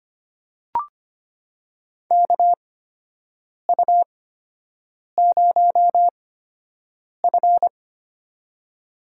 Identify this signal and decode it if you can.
Morse code, heard as a keyed tone, about 25 wpm, spaced as for 7 wpm, 700 Hz. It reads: KU0F
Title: USA Call Signs - 4 Characters - 8x Character Spacing - 25wpm